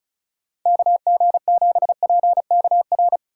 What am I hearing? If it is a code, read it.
Morse code, heard as a keyed tone, about 34 wpm, 700 Hz. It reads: KG7PKR
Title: USA Call Signs - 6 Characters - 3x Character Spacing - 35wpm